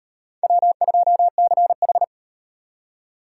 Morse code, heard as a keyed tone, about 33 wpm, 700 Hz. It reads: W2CH